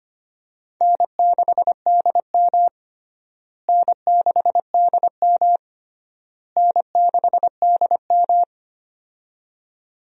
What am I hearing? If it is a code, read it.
Morse code, heard as a keyed tone, about 25 wpm, 700 Hz. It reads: N6DM N6DM N6DM
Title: USA Call Signs - 4 Characters - Repeated 3x - 25wpm